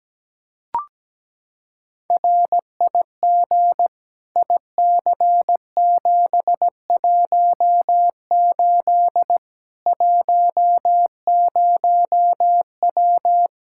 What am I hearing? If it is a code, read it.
Morse code, heard as a keyed tone, about 17 wpm, 700 Hz. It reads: RIG IC718 10W
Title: Practice QSO - Exchange 2 - Info - 17wpm